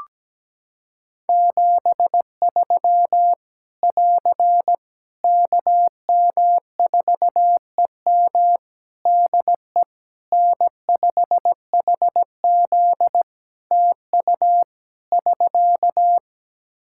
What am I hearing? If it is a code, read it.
Morse code, heard as a keyed tone, about 17 wpm, 700 Hz. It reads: 73 <AR> KM4EM DE N5HZ TU <SK>
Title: Practice QSO - Exchange 4 - Ending - 17wpm